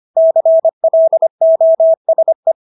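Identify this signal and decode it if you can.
Morse code, heard as a keyed tone, about 25 wpm, 650 Hz. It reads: CLOSE
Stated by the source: Morse practice of English words at 25 wpm from CW med Gnister.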